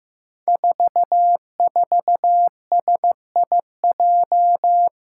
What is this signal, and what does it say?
Morse code, heard as a keyed tone, about 15 wpm, 700 Hz. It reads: 44SIJ
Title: Five Challenging Letter-Number Combinations 15wpm